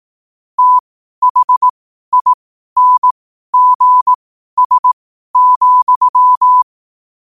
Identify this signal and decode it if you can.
Morse code, heard as a keyed tone, about 18 wpm, spaced as for 12 wpm, 1000 Hz. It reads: THINGS,
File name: qotdfiles / 12wpm